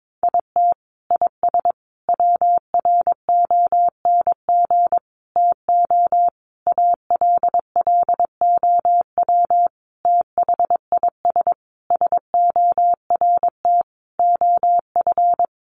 Morse code, heard as a keyed tone, about 22 wpm, 700 Hz. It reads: IT IS WRONG TO ALLOW THIS SORT OF
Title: Sentences from Top 1300 Words 22wpm